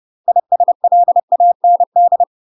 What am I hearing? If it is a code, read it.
Morse code, heard as a keyed tone, about 30 wpm, 700 Hz. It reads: ISLAND